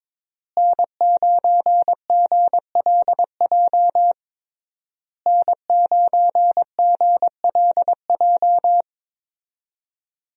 Morse code, heard as a keyed tone, about 22 wpm, 700 Hz. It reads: N9GLJ N9GLJ
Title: USA Call Signs - 5 Characters - Repeated 3x - 22wpm